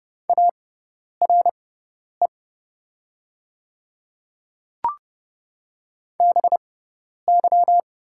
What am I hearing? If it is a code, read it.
Morse code, heard as a keyed tone, about 30 wpm, spaced as for 10 wpm, 700 Hz. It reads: ARE BY
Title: 6x Character Spacing - Top 300 Words - 30wpm